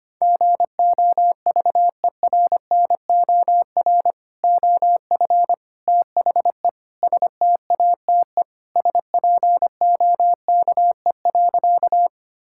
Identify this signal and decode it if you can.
Morse code, heard as a keyed tone, about 25 wpm, 700 Hz. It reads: GOVERNOR OF THE STATE SPOKE.
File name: practice / 25wpm / Sentences with 3-Word Phrases - 25wpm